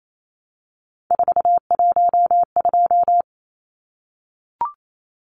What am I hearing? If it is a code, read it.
Morse code, heard as a keyed tone, about 28 wpm, 700 Hz. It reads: 412
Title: Numbers - 3-Digit 28wpm